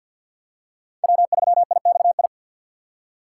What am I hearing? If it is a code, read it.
Morse code, heard as a keyed tone, about 48 wpm, 700 Hz. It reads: W3IXI
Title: USA Call Signs - 5 Characters - 4x Character Spacing - 50wpm